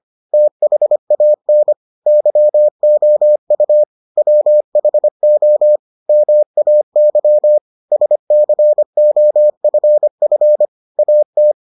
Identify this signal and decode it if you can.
Morse code, heard as a keyed tone, about 25 wpm, 600 Hz. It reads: THAN YOU WHO MAY SCOFF AT